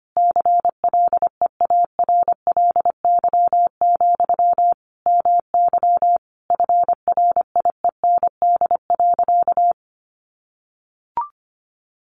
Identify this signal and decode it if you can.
Morse code, heard as a keyed tone, about 25 wpm, 700 Hz. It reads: CLEARLY, MY FRIEND.
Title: Sentences from Top 1300 Words 25wpm